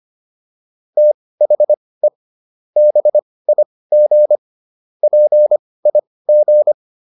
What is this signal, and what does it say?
Morse code, heard as a keyed tone, about 25 wpm, spaced as for 18 wpm, 600 Hz. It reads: THE BIG PIG
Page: intermediate-practice-files